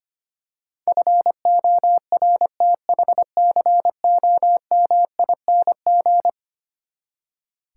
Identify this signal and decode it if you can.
Morse code, heard as a keyed tone, about 25 wpm, 700 Hz. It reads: FORTHCOMING
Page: morse-code-ninja-practice-25wpm